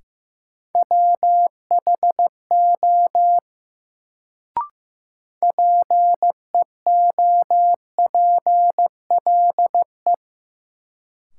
Morse code, heard as a keyed tone, about 15 wpm, 700 Hz. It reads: WHO PEOPLE
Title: Sets of 2 Words - New Word Drill - 15wpm